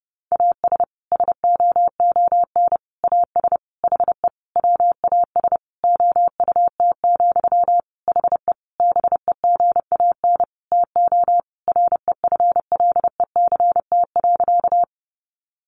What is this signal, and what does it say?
Morse code, heard as a keyed tone, about 30 wpm, 700 Hz. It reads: AS SOON AS HE WAS OUT, HE BEGAN TO REFLECT.
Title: Sentences from Top 1500 Words 30wpm